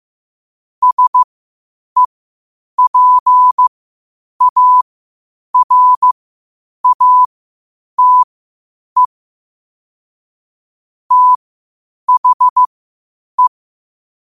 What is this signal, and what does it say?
Morse code, heard as a keyed tone, about 15 wpm, spaced as for 8 wpm, 1000 Hz. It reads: SEPARATE THE
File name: qotdfiles / 8wpm